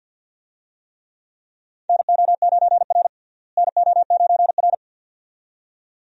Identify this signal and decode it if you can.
Morse code, heard as a keyed tone, about 52 wpm, 700 Hz. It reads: NO9R NO9R
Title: USA Call Signs - 4 Characters - Repeated 2x - 50wpm